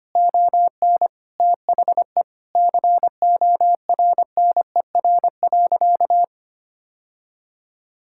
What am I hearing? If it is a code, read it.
Morse code, heard as a keyed tone, about 25 wpm, 700 Hz. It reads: ON THE CORNER.
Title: Sentences from Top 1600 Words 25wpm